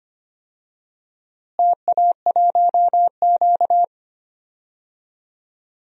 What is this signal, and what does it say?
Morse code, heard as a keyed tone, about 25 wpm, 700 Hz. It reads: TA1Q
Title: DXCC-100 Call Signs - 4 Characters - Repeated 2x - 25wpm